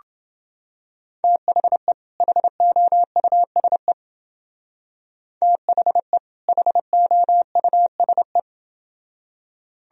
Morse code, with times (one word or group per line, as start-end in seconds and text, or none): 1.2-1.9 THE
2.2-3.9 HOUSE
5.4-6.2 THE
6.5-8.4 HOUSE